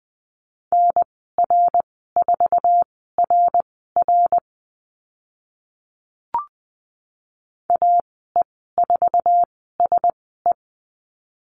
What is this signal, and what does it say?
Morse code, heard as a keyed tone, about 20 wpm, spaced as for 14 wpm, 700 Hz. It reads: NR4RR AE4SE
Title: 062 Lesson 11g - Letter h - Call Signs 20wpm